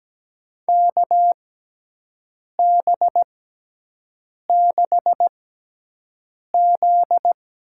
Morse code, heard as a keyed tone, about 17 wpm, spaced as for 6 wpm, 700 Hz. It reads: KB6Z
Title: USA Call Signs - 6x Character Spacing - 17wpm